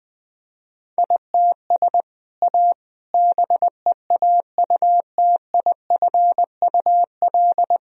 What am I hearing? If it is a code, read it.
Morse code, heard as a keyed tone, about 21 wpm, 700 Hz. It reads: ITS A BEAUTIFUL
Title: Sentences from Top 1000 Words - Review Part 1 - 20wpm